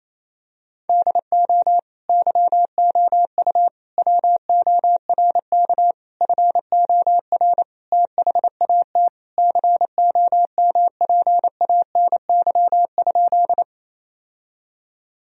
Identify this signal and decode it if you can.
Morse code, heard as a keyed tone, about 28 wpm, 700 Hz. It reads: DO YOU WORK FOR THAT COMPANY?
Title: Sentences from Top 1000 Words - Review Part 3 - 28wpm